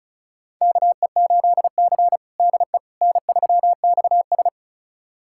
Morse code, heard as a keyed tone, about 34 wpm, 700 Hz. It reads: KE8C DE N3XS
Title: USA Callsign DE Callsign - 4 Characters - 35wpm